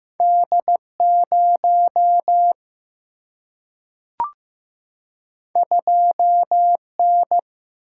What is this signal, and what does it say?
Morse code, heard as a keyed tone, about 15 wpm, 700 Hz. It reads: D 0 2 N
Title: Single-Letter Single-Number 15wpm